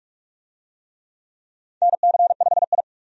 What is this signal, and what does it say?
Morse code, heard as a keyed tone, about 46 wpm, 700 Hz. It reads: NC5I
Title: USA Call Signs - 4 Characters - 45wpm